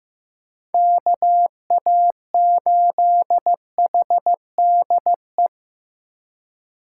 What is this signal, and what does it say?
Morse code, heard as a keyed tone, about 15 wpm, 700 Hz. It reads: KA8HDE
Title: K1USN SST QSO Practice - 15wpm